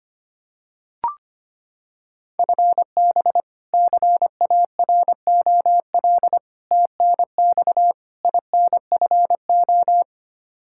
Morse code, heard as a keyed tone, about 25 wpm, 700 Hz. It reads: FB CAROL TNX INFO
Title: Practice QSO - Exchange 3 - More Info - 25wpm